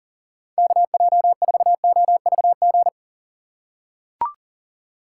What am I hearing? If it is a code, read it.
Morse code, heard as a keyed tone, about 37 wpm, 700 Hz. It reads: KJ4OVG
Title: USA Call Signs - 6 Characters - 6x Character Spacing - 40wpm